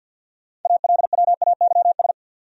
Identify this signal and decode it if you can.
Morse code, heard as a keyed tone, about 67 wpm, spaced as for 58 wpm, 700 Hz. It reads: ALWAYS
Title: Speed-Racing Top 500 Words 50wpm